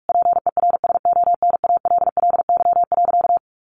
Morse code, heard as a keyed tone, about 44 wpm, 700 Hz. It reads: PERSONALLY.
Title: Sentences from Top 1400 Words 45wpm